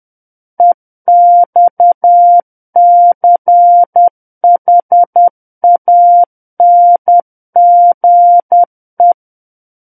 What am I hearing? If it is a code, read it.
Morse code, heard as a keyed tone, about 10 wpm, 700 Hz. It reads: EXCHANGE
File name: Book One / 01-Chapter 1 - 10 WPM